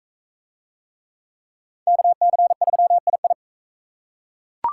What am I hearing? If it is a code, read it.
Morse code, heard as a keyed tone, about 39 wpm, 700 Hz. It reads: KC3II